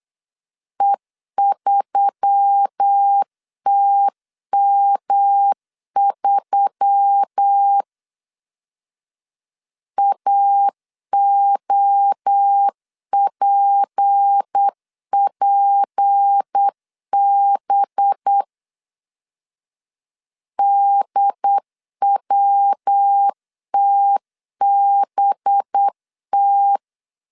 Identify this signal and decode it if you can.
Morse code, heard as a keyed tone, about 8 wpm, 800 Hz. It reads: E3TM3 AOPPB DWTBT
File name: CW / DET / 1 6 1500C S B 8wpm 5142